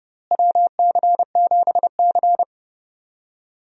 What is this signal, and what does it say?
Morse code, heard as a keyed tone, about 31 wpm, 700 Hz. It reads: WC7C
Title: USA Call Signs - 4 Characters - Repeated 3x - 30wpm